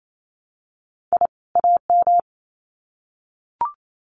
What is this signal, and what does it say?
Morse code, heard as a keyed tone, about 28 wpm, 700 Hz. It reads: I AM
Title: Sets of 2 Words 28wpm